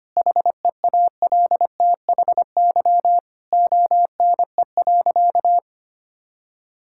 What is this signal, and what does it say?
Morse code, heard as a keyed tone, about 25 wpm, 700 Hz. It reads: HEALTHY ONE.
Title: Sentences from Top 1000 Words - Review Part 5 - 25wpm